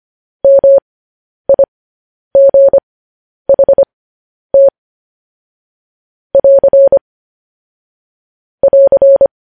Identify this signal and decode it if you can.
Morse code, heard as a keyed tone, about 25 wpm, spaced as for 10 wpm, 550 Hz. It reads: MIGHT <AR> <AR>
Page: news headlines in morse code at 10 wpm